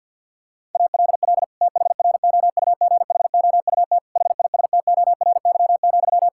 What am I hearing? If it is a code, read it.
Morse code, heard as a keyed tone, about 52 wpm, 700 Hz. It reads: ALL THROUGHOUT HISTORY,
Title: Sentences from Top 1000 Words - Review Part 5 - 50wpm